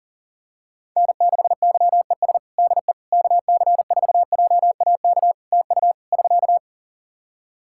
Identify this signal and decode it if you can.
Morse code, heard as a keyed tone, about 40 wpm, 700 Hz. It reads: N6YES DE KC4JAK TU <SK>
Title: K1USN SST QSO Practice 40wpm